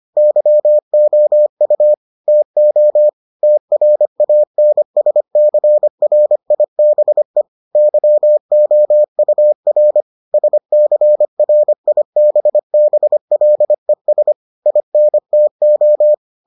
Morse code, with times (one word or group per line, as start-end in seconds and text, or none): 0.2-1.9 YOU
2.3-3.1 TO
3.4-7.4 TRANSCRIBE
7.8-10.0 YOUR
10.3-14.3 SCRIBBLES
14.7-16.1 INTO